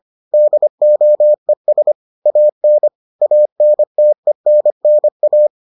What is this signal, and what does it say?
Morse code, heard as a keyed tone, about 25 wpm, 600 Hz. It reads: DOES AN ANTENNA